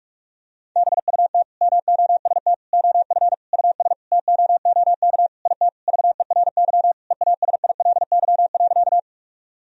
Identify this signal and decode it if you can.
Morse code, heard as a keyed tone, about 46 wpm, 700 Hz. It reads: BUT MOST OF US TOOK IT VERY EASILY.
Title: Sentences from Top 1000 Words - Review Part 1 - 45wpm